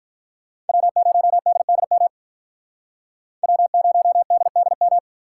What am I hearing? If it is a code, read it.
Morse code, heard as a keyed tone, about 43 wpm, 700 Hz. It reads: W0DDM W0DDM
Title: Speed Racing - USA 1x3 Call Signs 35wpm